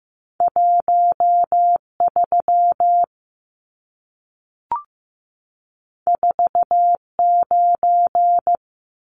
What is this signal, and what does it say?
Morse code, heard as a keyed tone, about 15 wpm, 700 Hz. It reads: 1 3 4 9
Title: Numbers - 2-Digit 15wpm